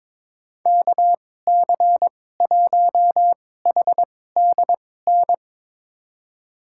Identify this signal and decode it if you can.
Morse code, heard as a keyed tone, about 22 wpm, spaced as for 16 wpm, 700 Hz. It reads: KC1HDN
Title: USA Call Signs - 6 Characters - 2x Character Spacing - 22wpm